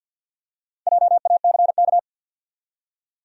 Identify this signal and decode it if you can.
Morse code, heard as a keyed tone, about 52 wpm, 700 Hz. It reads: JACK
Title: USA Common Names - Max 5 Letters - 50wpm